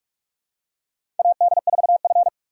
Morse code, heard as a keyed tone, about 46 wpm, 700 Hz. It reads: AD4F
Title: USA Call Signs - 2x Character Spacing - 45wpm